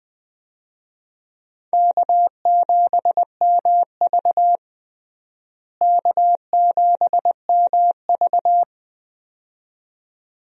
Morse code, heard as a keyed tone, about 20 wpm, 700 Hz. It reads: K7MV K7MV
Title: USA Call Signs - 4 Characters - Repeated 2x - 20wpm